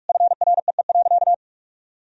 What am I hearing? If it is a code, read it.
Morse code, heard as a keyed tone, about 41 wpm, 700 Hz. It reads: FREE.
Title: Sentences from Top 1000 Words - Review Part 1 - 45wpm